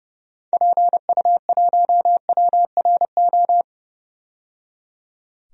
Morse code, heard as a keyed tone, about 30 wpm, 700 Hz. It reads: PU1WRO
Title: DXCC-100 Call Signs - 6 Characters - 30wpm